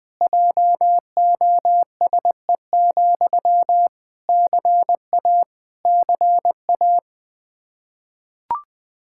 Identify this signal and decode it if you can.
Morse code, heard as a keyed tone, about 21 wpm, 700 Hz. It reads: JOSE, CA CA
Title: Practice QSO - Exchange 1 - Beginning - 20wpm